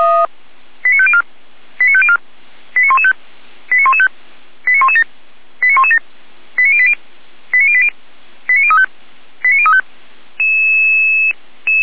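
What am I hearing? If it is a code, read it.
Five-tone selective call (ZVEI), heard as quick runs of five tones, tones of about 70 ms. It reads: 89504 89504 89115 89115 89117 89117 89080 89080 89035 89035
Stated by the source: Five-tone ZVEI selective call received on an SDR.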